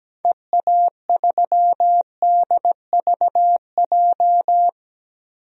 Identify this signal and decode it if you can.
Morse code, heard as a keyed tone, about 17 wpm, 700 Hz. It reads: EA3DVJ
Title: DXCC-100 Call Signs - 6 Characters - 17wpm